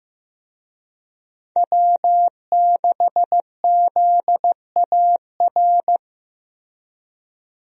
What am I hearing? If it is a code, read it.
Morse code, heard as a keyed tone, about 15 wpm, 700 Hz. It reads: W6ZAR
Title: USA Call Signs - 5 Characters - Repeated 2x - 15wpm